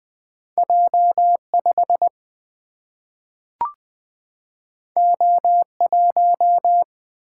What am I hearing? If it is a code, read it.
Morse code, heard as a keyed tone, about 20 wpm, 700 Hz. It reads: J5 O1
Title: Single-Letter Single-Number 20wpm